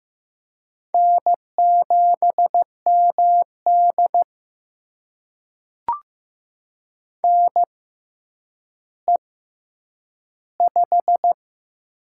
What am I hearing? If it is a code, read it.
Morse code, heard as a keyed tone, about 15 wpm, spaced as for 8 wpm, 700 Hz. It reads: N7MD NE5
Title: USA Call Signs - 4 Characters - 6x Character Spacing - 15wpm